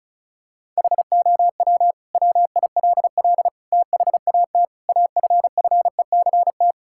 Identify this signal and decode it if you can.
Morse code, heard as a keyed tone, about 34 wpm, 700 Hz. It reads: HOW WILL THAT AFFECT
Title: Sentences from Top 1000 Words - Review Part 6 - 35wpm